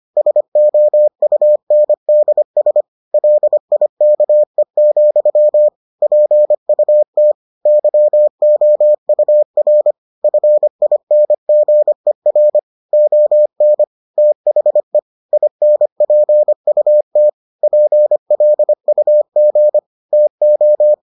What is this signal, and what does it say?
Morse code, heard as a keyed tone, about 25 wpm, 600 Hz. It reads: SOUNDS LIKE, PUT YOUR FINGER ON THE INPUT PLUG TO